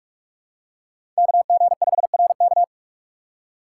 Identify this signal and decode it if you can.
Morse code, heard as a keyed tone, about 44 wpm, 700 Hz. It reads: KG5RK
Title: USA Call Signs - 5 Characters - 6x Character Spacing - 45wpm